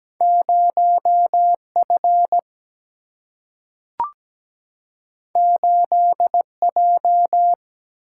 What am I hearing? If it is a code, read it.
Morse code, heard as a keyed tone, about 17 wpm, 700 Hz. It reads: 0F 8J
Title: Single-Letter Single-Number 17wpm